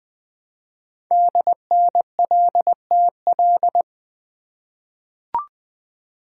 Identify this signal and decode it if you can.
Morse code, heard as a keyed tone, about 20 wpm, 700 Hz. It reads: DNLTL
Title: Code Groups - 5 Letters - 20wpm